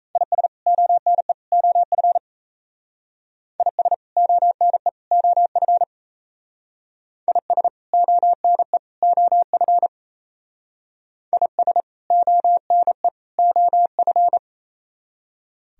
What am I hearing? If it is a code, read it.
Morse code, heard as a keyed tone, about 34 wpm, 700 Hz. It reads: IS ONE OF IS ONE OF IS ONE OF IS ONE OF
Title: Speed-Racing - Sets of 3 Words 28wpm